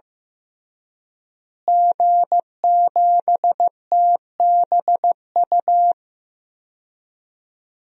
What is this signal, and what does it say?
Morse code, heard as a keyed tone, about 15 wpm, 700 Hz. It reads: G7TBU
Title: DXCC-100 Call Signs - 5 Characters - Repeated 3x - 15wpm